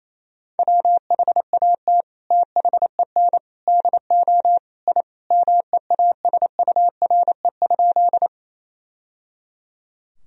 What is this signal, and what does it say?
Morse code, heard as a keyed tone, about 28 wpm, 700 Hz. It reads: WHAT THEN DO I MEASURE?
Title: Sentences from Top 1000 Words - Review Part 2 - 28wpm